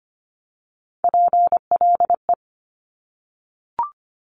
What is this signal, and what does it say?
Morse code, heard as a keyed tone, about 25 wpm, 700 Hz. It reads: PLE
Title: Top Three Letter Combinations in Words 25wpm